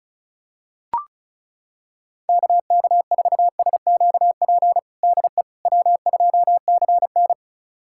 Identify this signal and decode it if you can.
Morse code, heard as a keyed tone, about 36 wpm, 700 Hz. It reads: KK4SQP DE W2CN
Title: USA Callsign DE Callsign 35wpm